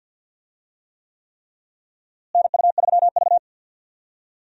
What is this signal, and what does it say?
Morse code, heard as a keyed tone, about 51 wpm, 700 Hz. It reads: NU3V
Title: USA Call Signs - 4 Characters - Repeated 3x - 50wpm